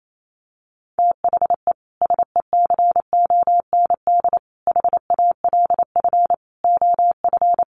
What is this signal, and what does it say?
Morse code, heard as a keyed tone, about 28 wpm, 700 Hz. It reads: THE SECOND HALF OF
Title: Sets of 5 Words 28wpm